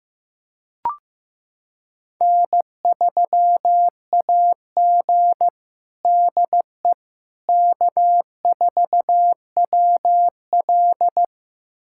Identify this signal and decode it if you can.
Morse code, heard as a keyed tone, about 15 wpm, 700 Hz. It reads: N3AG DE K4WL
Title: USA Callsign DE Callsign - 4 Characters - 15wpm